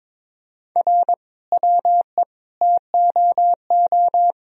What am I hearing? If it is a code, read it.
Morse code, heard as a keyed tone, about 22 wpm, 700 Hz. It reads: R WE TOO
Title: Sentences from Top 300 Words and Abbreviations 22wpm